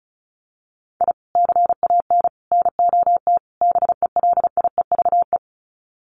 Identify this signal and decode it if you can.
Morse code, heard as a keyed tone, about 34 wpm, 700 Hz. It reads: I CAN NOT BELIEVE
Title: Sets of 4 Words 35wpm